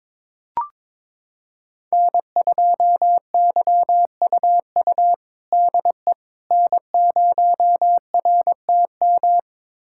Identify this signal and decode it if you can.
Morse code, heard as a keyed tone, about 21 wpm, 700 Hz. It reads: N2YUU DE N0RTM
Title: USA Callsign DE Callsign - 5 Characters - 22wpm